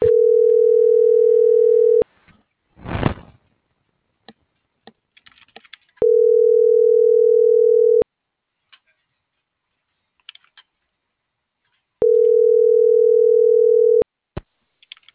Ringback tone (North America), 440 + 480 Hz.